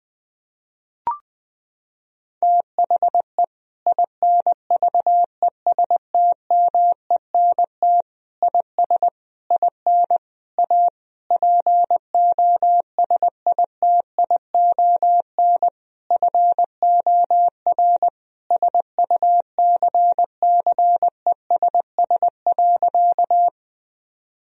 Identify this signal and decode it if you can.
Morse code, heard as a keyed tone, about 20 wpm, 700 Hz. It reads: THE INVESTMENT IS IN A POSITION FOR SUCCESS.
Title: Sentences from Top 1000 Words - Review Part 4 - 20wpm